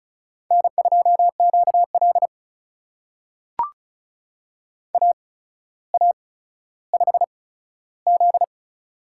Morse code, heard as a keyed tone, about 36 wpm, spaced as for 15 wpm, 700 Hz. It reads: N2QL AA5Z
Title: USA Call Signs - 4 Characters - 8x Character Spacing - 35wpm